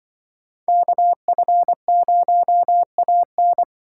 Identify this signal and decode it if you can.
Morse code, heard as a keyed tone, about 25 wpm, 700 Hz. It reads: KF0AN
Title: Speed Racing - USA 2x2 Call Signs 20wpm